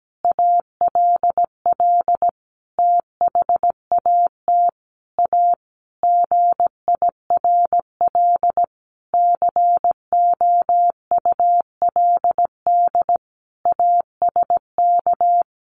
Morse code, heard as a keyed tone, about 17 wpm, 700 Hz. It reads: ALL THAT A GIRL COULD ASK
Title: Sentences from Top 1400 Words 17wpm